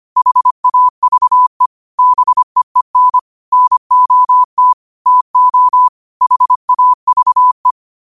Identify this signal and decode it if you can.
Morse code, heard as a keyed tone, about 25 wpm, 1000 Hz. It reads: HAVE BEEN NOT TO HAVE